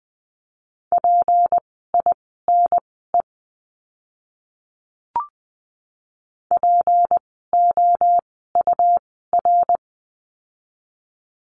Morse code, heard as a keyed tone, about 20 wpm, spaced as for 14 wpm, 700 Hz. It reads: PINE POUR
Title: 153 Lesson 25f - Letter p - Words 20wpm